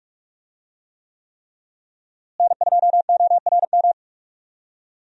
Morse code, heard as a keyed tone, about 46 wpm, 700 Hz. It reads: N2ORM
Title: USA Call Signs - 5 Characters - Repeated 3x - 45wpm